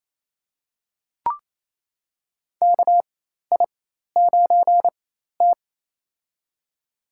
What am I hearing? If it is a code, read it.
Morse code, heard as a keyed tone, about 28 wpm, spaced as for 13 wpm, 700 Hz. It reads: KI9T
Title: USA Call Signs - 4 Characters - 4x Character Spacing - 28wpm